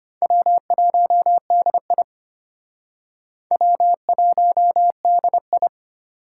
Morse code, heard as a keyed tone, about 32 wpm, 700 Hz. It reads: W1DI W1DI